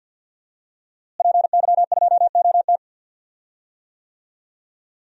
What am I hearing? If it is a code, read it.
Morse code, heard as a keyed tone, about 48 wpm, 700 Hz. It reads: PY2OT
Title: DXCC-100 Call Signs - 5 Characters - Repeated 3x - 50wpm